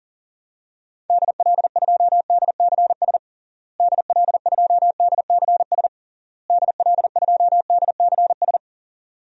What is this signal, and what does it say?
Morse code, heard as a keyed tone, about 40 wpm, 700 Hz. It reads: DL2DCS DL2DCS DL2DCS
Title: DXCC-100 Call Signs - 6 Characters - Repeated 3x - 40wpm